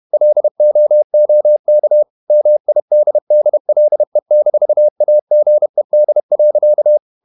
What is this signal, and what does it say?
Morse code, heard as a keyed tone, about 31 wpm, 600 Hz. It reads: LOOK MIDDLE-AGED.
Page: Never 31wpm | CW med Gnister